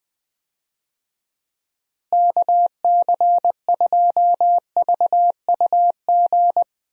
Morse code, heard as a keyed tone, about 21 wpm, 700 Hz. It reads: KC2VUG
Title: USA Call Signs - 6 Characters - Repeated 3x - 20wpm